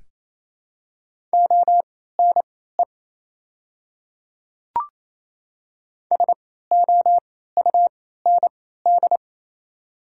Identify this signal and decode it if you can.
Morse code, heard as a keyed tone, about 28 wpm, spaced as for 16 wpm, 700 Hz. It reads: ONE SOUND
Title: 3x Character Spacing - Top 100 Words - 28wpm